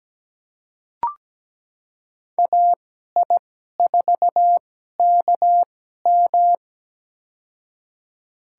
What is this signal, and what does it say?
Morse code, heard as a keyed tone, about 17 wpm, spaced as for 12 wpm, 700 Hz. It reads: AI4KM
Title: USA Call Signs - 5 Characters - 2x Character Spacing - 17wpm